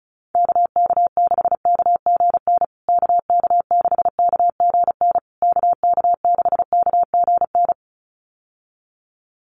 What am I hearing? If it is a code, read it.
Morse code, heard as a keyed tone, about 34 wpm, 700 Hz. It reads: KK6KGN KK6KGN KK6KGN
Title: USA Call Signs Repeated 3x 35wpm